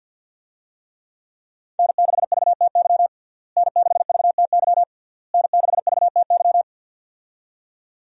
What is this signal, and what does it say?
Morse code, heard as a keyed tone, about 52 wpm, 700 Hz. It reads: N6VTY N6VTY N6VTY
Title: USA Call Signs - 5 Characters - Repeated 3x - 50wpm